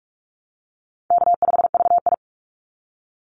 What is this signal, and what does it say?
Morse code, heard as a keyed tone, about 44 wpm, 700 Hz. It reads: K5VI
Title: Call Signs Repeated 2x - 45wpm